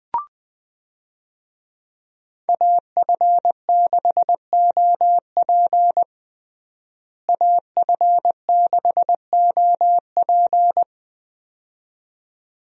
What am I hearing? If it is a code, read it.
Morse code, heard as a keyed tone, about 21 wpm, 700 Hz. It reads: AF6OP AF6OP
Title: USA Call Signs - 5 Characters - Repeated 2x - 20wpm